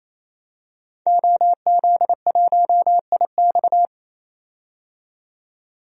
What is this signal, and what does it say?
Morse code, heard as a keyed tone, about 28 wpm, 700 Hz. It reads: OZ1IX
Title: DXCC-100 Call Signs - 5 Characters - 28wpm